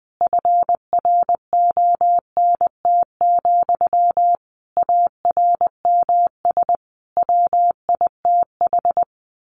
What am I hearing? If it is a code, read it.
Morse code, heard as a keyed tone, about 20 wpm, 700 Hz. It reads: FRONT, ARMS WITH
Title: A Sound Of Thunder 20wpm